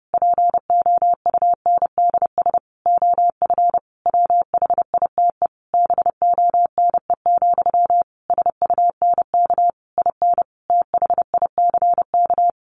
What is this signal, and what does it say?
Morse code, heard as a keyed tone, about 30 wpm, 700 Hz. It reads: POUNDS OF WHITE BONE, SUNK IN THICK
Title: A Sound Of Thunder 30wpm